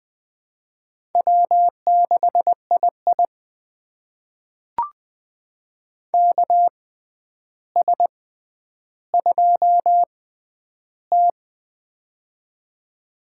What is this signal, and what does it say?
Morse code, heard as a keyed tone, about 20 wpm, spaced as for 10 wpm, 700 Hz. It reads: W6II KS2T
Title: USA Call Signs - 6x Character Spacing - 20wpm